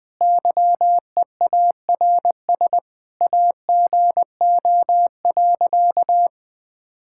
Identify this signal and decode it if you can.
Morse code, heard as a keyed tone, about 20 wpm, 700 Hz. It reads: YEARS AGO.
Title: Sentences with 4-Word Phrases - 20wpm